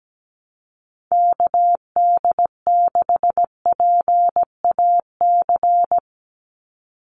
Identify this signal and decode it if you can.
Morse code, heard as a keyed tone, about 17 wpm, 700 Hz. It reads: KD6PAC
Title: USA Call Signs 17wpm